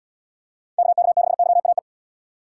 Morse code, heard as a keyed tone, about 78 wpm, 700 Hz. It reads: KK6YAE